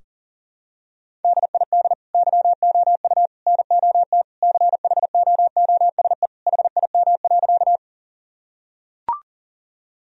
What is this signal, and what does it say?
Morse code, heard as a keyed tone, about 40 wpm, 700 Hz. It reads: DID YOU NOT CHOOSE HIM.
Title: Sentences from Top 1000 Words - Review Part 4 - 40wpm